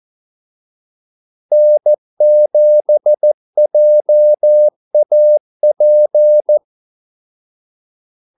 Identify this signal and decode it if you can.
Morse code, heard as a keyed tone, about 14 wpm, 600 Hz. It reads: N7JAP
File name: Callsigns-14wpm